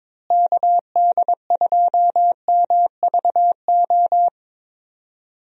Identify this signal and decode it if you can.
Morse code, heard as a keyed tone, about 22 wpm, 700 Hz. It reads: KD2MVO
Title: USA Call Signs - 6 Characters - Repeated 2x - 22wpm